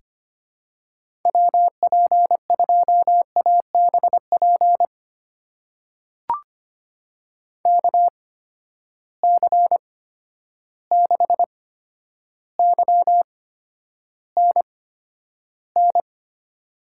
Morse code, heard as a keyed tone, about 25 wpm, spaced as for 11 wpm, 700 Hz. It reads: WP2ABP KC6YNN
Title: USA Call Signs - 6 Characters - 8x Character Spacing - 25wpm